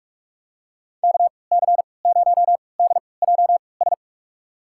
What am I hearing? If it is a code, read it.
Morse code, heard as a keyed tone, about 41 wpm, spaced as for 25 wpm, 700 Hz. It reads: KC0DJS
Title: USA Call Signs - 6 Characters - 3x Character Spacing - 45wpm